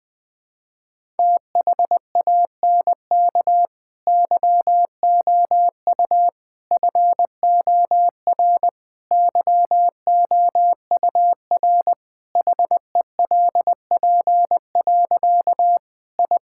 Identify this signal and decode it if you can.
Morse code, heard as a keyed tone, about 20 wpm, 700 Hz. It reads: THANK YOU FOR YOUR HELP. I
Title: Difficult Sentences with Hints - 20wpm